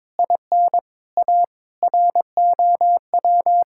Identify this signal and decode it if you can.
Morse code, heard as a keyed tone, about 22 wpm, 700 Hz. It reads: IN A ROW
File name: practice / 22wpm / Sets of 6 Words - 22wpm